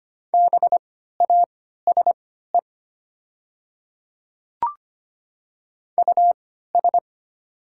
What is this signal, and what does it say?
Morse code, heard as a keyed tone, about 25 wpm, spaced as for 14 wpm, 700 Hz. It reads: BASE US
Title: 3x Character Spacing - Top 500 Words - 25wpm